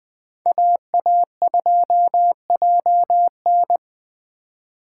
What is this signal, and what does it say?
Morse code, heard as a keyed tone, about 20 wpm, 700 Hz. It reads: AA2JN